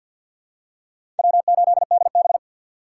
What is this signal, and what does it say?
Morse code, heard as a keyed tone, about 53 wpm, 700 Hz. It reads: W8DB
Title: USA Call Signs - 4 Characters - 2x Character Spacing - 50wpm